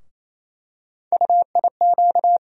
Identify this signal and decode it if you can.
Morse code, heard as a keyed tone, about 28 wpm, 700 Hz. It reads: UIQ
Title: USA Call Sign Postfixes - 3 Letters 28wpm